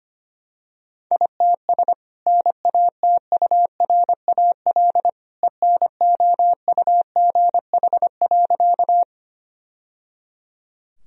Morse code, heard as a keyed tone, about 25 wpm, 700 Hz. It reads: ITS NATURAL ENOUGH.
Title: Sentences from Top 1000 Words - Review Part 4 - 25wpm